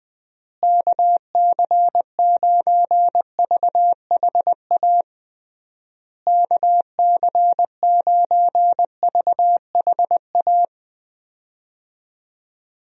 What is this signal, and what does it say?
Morse code, heard as a keyed tone, about 20 wpm, 700 Hz. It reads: KC9VHA KC9VHA
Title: USA Call Signs - 6 Characters - Repeated 3x - 20wpm